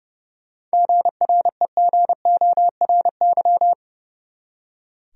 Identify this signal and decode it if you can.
Morse code, heard as a keyed tone, about 30 wpm, 700 Hz. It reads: GREGORY
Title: USA Common Names - 30wpm